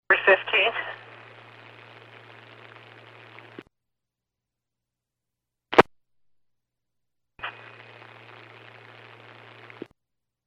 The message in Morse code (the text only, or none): none